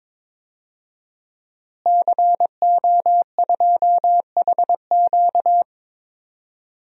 Morse code, heard as a keyed tone, about 23 wpm, 700 Hz. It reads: CO2HQ
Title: DXCC-100 Call Signs - 5 Characters - Repeated 2x - 22wpm